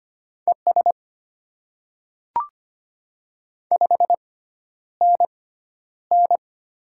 Morse code, heard as a keyed tone, about 25 wpm, spaced as for 11 wpm, 700 Hz. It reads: ES 5NN
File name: Top 20 QSO Elements - 6x Character Spacing - 25wpm